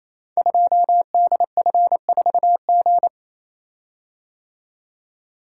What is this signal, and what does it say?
Morse code, heard as a keyed tone, about 28 wpm, 700 Hz. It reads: 2DF4G
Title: Code Groups - 5 Letter:Numbers - 28wpm